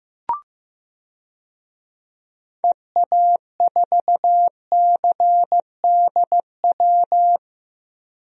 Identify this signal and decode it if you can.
Morse code, heard as a keyed tone, about 16 wpm, 700 Hz. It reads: EA4CDW
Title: DXCC-100 Call Signs - 6 Characters - Repeated 3x - 15wpm